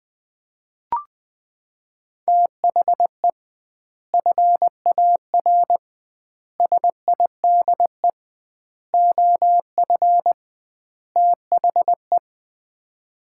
Morse code, heard as a keyed tone, about 20 wpm, 700 Hz. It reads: THE FAR SIDE OF THE
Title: 2x Word-Spacing - Sets of 5 Words - 20wpm